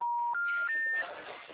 Special information tone, with noise.
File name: out- -2002